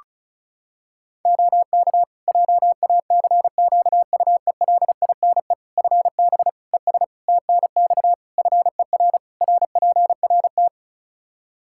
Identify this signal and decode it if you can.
Morse code, heard as a keyed tone, about 36 wpm, 700 Hz. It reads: OK JACQUELINE FB ES TNX FER RPRT
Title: K1USN SST QSO Practice 35wpm